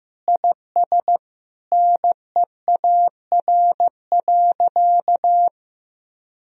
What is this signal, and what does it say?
Morse code, heard as a keyed tone, about 15 wpm, 700 Hz. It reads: IS NEAR.
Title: Sentences from Top 1000 Words - Review Part 1 - 15wpm